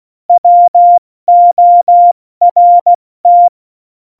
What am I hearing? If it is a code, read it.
Morse code, heard as a keyed tone, about 16 wpm, spaced as for 14 wpm, 700 Hz. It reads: WORT